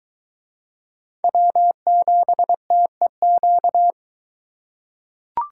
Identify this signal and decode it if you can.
Morse code, heard as a keyed tone, about 22 wpm, 700 Hz. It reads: W7TEQ